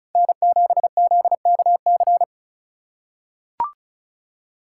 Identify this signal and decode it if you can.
Morse code, heard as a keyed tone, about 34 wpm, 700 Hz. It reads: N7ZKC
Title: USA Call Signs - 4x Character Spacing - 35wpm